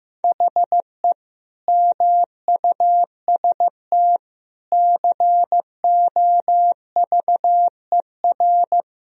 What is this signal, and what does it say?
Morse code, heard as a keyed tone, about 16 wpm, 700 Hz. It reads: HE MUST COVER
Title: Sentences from Top 500 Words and Abbreviations 15wpm